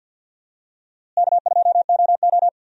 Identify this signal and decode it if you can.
Morse code, heard as a keyed tone, about 48 wpm, 700 Hz. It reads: K2OO
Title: USA Call Signs - 4 Characters - Repeated 2x - 50wpm